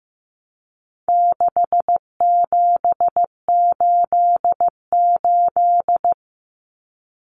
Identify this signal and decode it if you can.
Morse code, heard as a keyed tone, about 15 wpm, 700 Hz. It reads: 6788
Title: Numbers - 4-Digit 15wpm